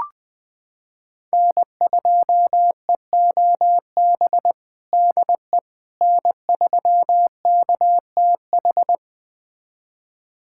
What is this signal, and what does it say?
Morse code, heard as a keyed tone, about 20 wpm, 700 Hz. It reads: N2EOB DE N3KTH
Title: USA Callsign DE Callsign - 5 Characters - 20wpm